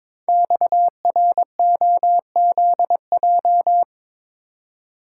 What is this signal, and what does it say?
Morse code, heard as a keyed tone, about 22 wpm, 700 Hz. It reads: XROZJ
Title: Code Groups - 5 Letters - 22wpm